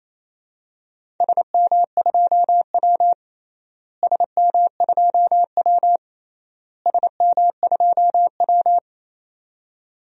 Morse code, heard as a keyed tone, about 28 wpm, 700 Hz. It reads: SM2W SM2W SM2W
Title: DXCC-100 Call Signs - 4 Characters - Repeated 3x - 28wpm